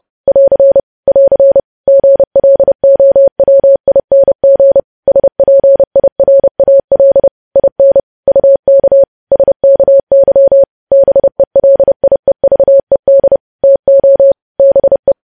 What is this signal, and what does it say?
Morse code, heard as a keyed tone, about 30 wpm, 550 Hz. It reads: <AR> <AR> GLOWING SPIRAL IN UK SKY BELIEVED TO BE
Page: news headlines in morse code at 30 wpm